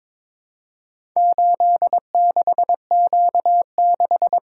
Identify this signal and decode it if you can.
Morse code, heard as a keyed tone, about 23 wpm, 700 Hz. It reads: 86Q6
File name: practice / 22wpm / Four Letter-Number Combinations - 22wpm